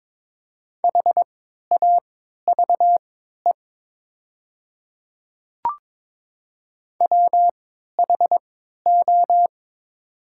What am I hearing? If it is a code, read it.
Morse code, heard as a keyed tone, about 22 wpm, spaced as for 13 wpm, 690 Hz. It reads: HAVE WHO
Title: 3x Character Spacing - Top 100 Words - 22wpm